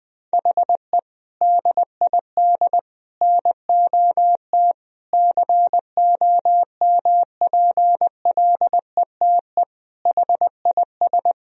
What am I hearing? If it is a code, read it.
Morse code, heard as a keyed tone, about 20 wpm, 700 Hz. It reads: HE DID NOT COMPLETE HIS
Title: Sentences from Top 1000 Words - Review Part 1 - 20wpm